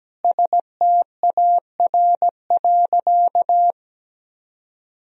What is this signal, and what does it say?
Morse code, heard as a keyed tone, about 17 wpm, 700 Hz. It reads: STAR.
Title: Sentences from Top 1000 Words - Review Part 3 - 17wpm